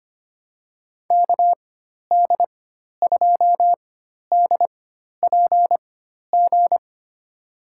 Morse code, heard as a keyed tone, about 25 wpm, spaced as for 12 wpm, 700 Hz. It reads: KD2DPG
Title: USA Call Signs - 6 Characters - 4x Character Spacing - 25wpm